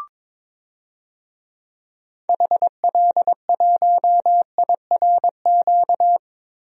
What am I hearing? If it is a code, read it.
Morse code, heard as a keyed tone, about 22 wpm, 700 Hz. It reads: HL1IRQ
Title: DXCC-100 Call Signs - 6 Characters - 22wpm